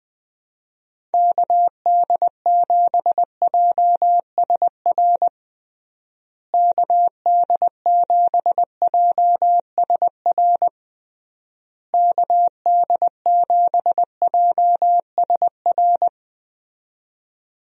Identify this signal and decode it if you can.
Morse code, heard as a keyed tone, about 20 wpm, 700 Hz. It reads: KD7JSR KD7JSR KD7JSR
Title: USA Call Signs - 6 Characters - Repeated 3x - 20wpm